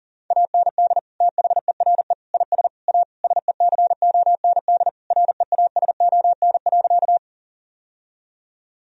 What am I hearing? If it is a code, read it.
Morse code, heard as a keyed tone, about 40 wpm, 700 Hz. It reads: AND THERE IS A SECOND REASON.
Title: Sentences from Top 1000 Words - Review Part 1 - 40wpm